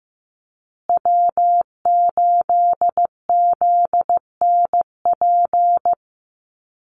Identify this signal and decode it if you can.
Morse code, heard as a keyed tone, about 15 wpm, 700 Hz. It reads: W8ZNP
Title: USA Call Signs Repeated 3x 15wpm